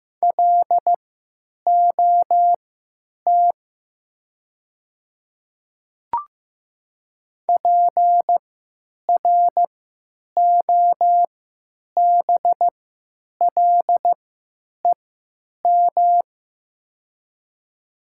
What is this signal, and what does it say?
Morse code, heard as a keyed tone, about 15 wpm, spaced as for 8 wpm, 700 Hz. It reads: LOT PROBLEM
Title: 3x Character Spacing - Top 500 Words - 15wpm